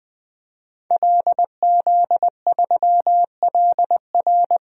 Morse code, heard as a keyed tone, about 21 wpm, 700 Hz. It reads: LZ3LR
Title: DXCC-100 Call Signs - 5 Characters - 20wpm